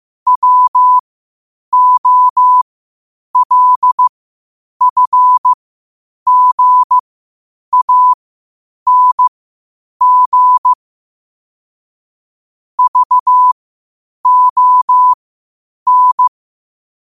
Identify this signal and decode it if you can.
Morse code, heard as a keyed tone, about 15 wpm, spaced as for 8 wpm, 1000 Hz. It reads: WOLFGANG VON
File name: qotdfiles / 8wpm